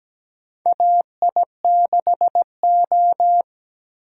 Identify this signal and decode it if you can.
Morse code, heard as a keyed tone, about 17 wpm, 700 Hz. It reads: AI6O
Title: USA Call Signs - 8x Character Spacing - 17wpm